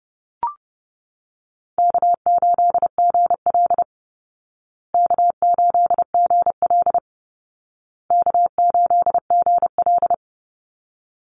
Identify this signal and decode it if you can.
Morse code, heard as a keyed tone, about 30 wpm, 700 Hz. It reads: K8GL K8GL K8GL
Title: Easy Call Signs Repeated 3x - 30wpm